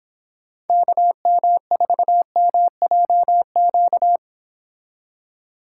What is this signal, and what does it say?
Morse code, heard as a keyed tone, about 26 wpm, 700 Hz. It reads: KM4MJQ